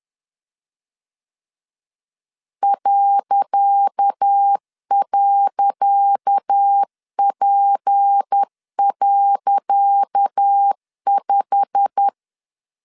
Morse code, heard as a keyed tone, about 10 wpm, 800 Hz. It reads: ..P.5
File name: LEZIONI / 10 SIGNS / 10C DOT 10wpm 2940